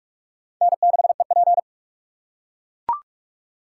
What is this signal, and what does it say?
Morse code, heard as a keyed tone, about 44 wpm, 700 Hz. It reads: N6EP
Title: USA Call Signs - 4 Characters - 8x Character Spacing - 45wpm